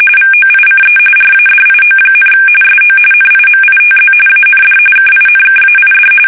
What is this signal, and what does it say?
RTTY, about 45.45 baud. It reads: VOICE RADIO, SO NIMITZ AND KINKAID
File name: fsk850 / battleshipnj